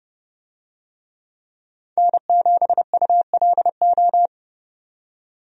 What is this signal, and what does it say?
Morse code, heard as a keyed tone, about 30 wpm, 700 Hz. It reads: N7ULO
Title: USA Call Signs - 5 Characters - 30wpm